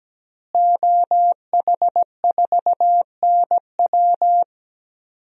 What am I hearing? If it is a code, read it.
Morse code, heard as a keyed tone, about 17 wpm, 700 Hz. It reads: OH4NW
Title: DXCC-100 Call Signs - 5 Characters - Repeated 3x - 17wpm